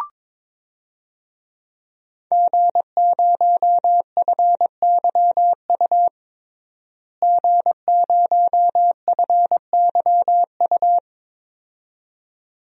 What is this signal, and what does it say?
Morse code, heard as a keyed tone, about 22 wpm, 700 Hz. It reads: G0FYU G0FYU
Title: DXCC-100 Call Signs - 5 Characters - Repeated 2x - 22wpm